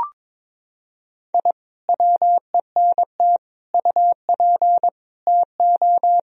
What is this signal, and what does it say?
Morse code, heard as a keyed tone, about 22 wpm, 700 Hz. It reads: I WENT UP TO
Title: Sentences from Top 1000 Words - Review Part 1 - 22wpm